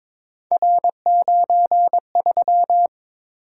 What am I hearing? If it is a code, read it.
Morse code, heard as a keyed tone, about 22 wpm, 700 Hz. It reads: R93
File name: practice / 22wpm / Three Letter-Number Combinations - 22wpm